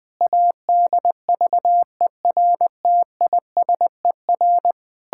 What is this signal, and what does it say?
Morse code, heard as a keyed tone, about 21 wpm, 700 Hz. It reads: ADVERTISER
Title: Pangram Challenge v2 - 20wpm